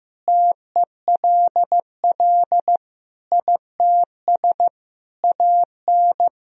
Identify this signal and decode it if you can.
Morse code, heard as a keyed tone, about 15 wpm, 700 Hz. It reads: TELL ITS AN
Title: Sentences from Top 1000 Words - Review Part 2 - 15wpm